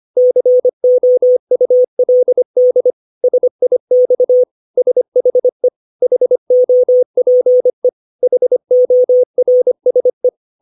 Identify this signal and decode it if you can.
Morse code, heard as a keyed tone, about 25 wpm, 500 Hz. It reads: COULD SIX SHE HOPE HORSE
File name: 632 words / 632 Words 25WPM